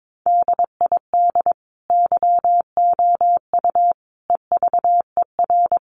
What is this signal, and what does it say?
Morse code, heard as a keyed tone, about 22 wpm, 700 Hz. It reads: DID YOU EVER
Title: Sentences from Top 400 Words 22wpm